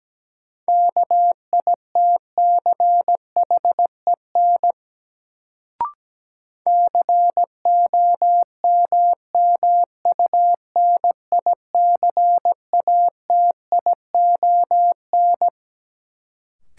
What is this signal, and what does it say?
Morse code, heard as a keyed tone, about 17 wpm, 700 Hz. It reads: KITCHEN COMMUNICATION
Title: Sentences from Top 1400 Words - New Word Drill Alt - 17wpm